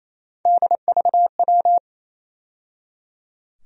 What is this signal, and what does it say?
Morse code, heard as a keyed tone, about 27 wpm, 700 Hz. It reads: DVW
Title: USA Call Sign Postfixes - 3 Letters 28wpm